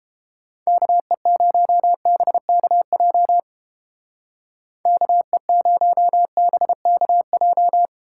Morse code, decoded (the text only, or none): KE0BKJ KE0BKJ